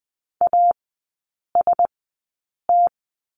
Morse code, heard as a keyed tone, about 20 wpm, spaced as for 8 wpm, 700 Hz. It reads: AST
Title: 047 Lesson 9e - Number 4 - Three Characters 20wpm